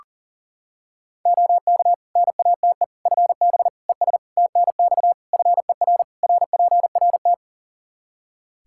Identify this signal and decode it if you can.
Morse code, heard as a keyed tone, about 40 wpm, 700 Hz. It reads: OK NATE FB ES TNX FER RPRT
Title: Practice QSO - Exchange 2 - Info - 40wpm